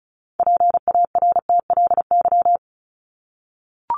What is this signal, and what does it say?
Morse code, heard as a keyed tone, about 34 wpm, 700 Hz. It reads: PARTLY
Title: LY Suffix - Top 100 Words with Base Words 35wpm